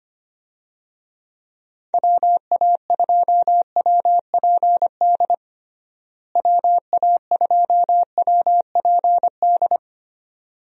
Morse code, heard as a keyed tone, about 25 wpm, 700 Hz. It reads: WA2WPD WA2WPD
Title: USA Call Signs - 6 Characters - Repeated 2x - 25wpm